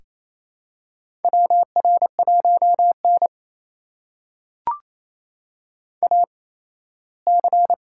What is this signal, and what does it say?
Morse code, heard as a keyed tone, about 28 wpm, spaced as for 17 wpm, 700 Hz. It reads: WR1N AC
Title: USA Call Signs - 4 Characters - 8x Character Spacing - 28wpm